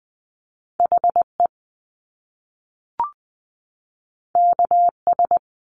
Morse code, heard as a keyed tone, about 20 wpm, 700 Hz. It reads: H E K S
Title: Top Two Letter Combinations in Words 20wpm